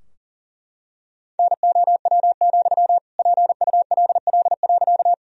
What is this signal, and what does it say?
Morse code, heard as a keyed tone, about 37 wpm, 700 Hz. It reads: NOW, PULL.
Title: Sentences from Top 1000 Words - Review Part 2 - 40wpm